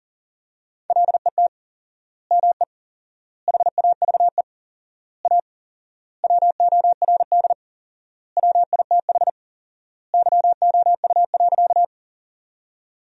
Morse code, heard as a keyed tone, about 40 wpm, 700 Hz. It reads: LET ME HAVE A WORD WITH YOU.
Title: 4x Word-Spacing - Sentences from Top 300 Words - 40wpm